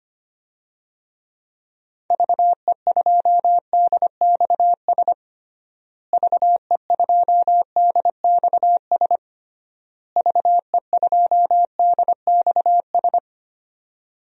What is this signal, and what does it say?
Morse code, heard as a keyed tone, about 25 wpm, 700 Hz. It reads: VE2DXS VE2DXS VE2DXS
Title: DXCC-100 Call Signs - 6 Characters - Repeated 3x - 25wpm